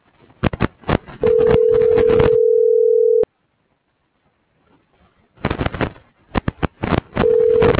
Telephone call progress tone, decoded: ringback tone

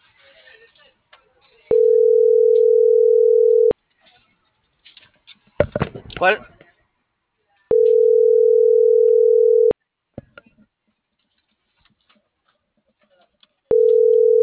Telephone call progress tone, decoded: ringback tone

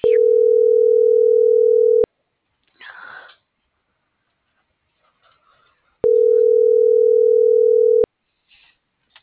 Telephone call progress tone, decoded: ringback tone

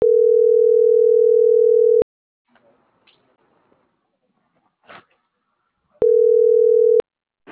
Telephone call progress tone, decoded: ringback tone

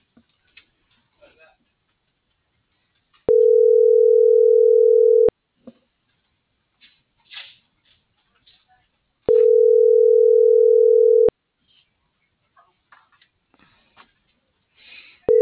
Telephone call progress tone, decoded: ringback tone

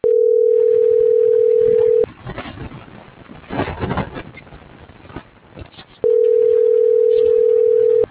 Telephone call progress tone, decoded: ringback tone

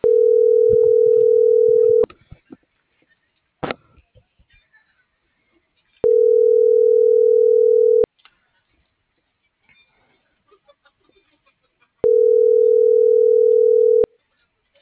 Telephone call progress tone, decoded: ringback tone

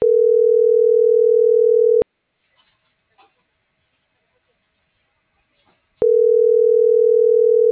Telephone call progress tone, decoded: ringback tone

